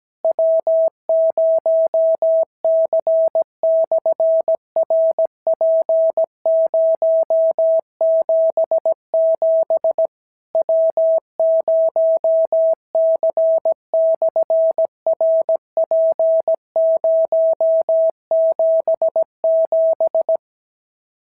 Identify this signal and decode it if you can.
Morse code, heard as a keyed tone, about 17 wpm, 650 Hz. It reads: W0C/RP077 W0C/RP077
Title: SOTA QSO Practice 17wpm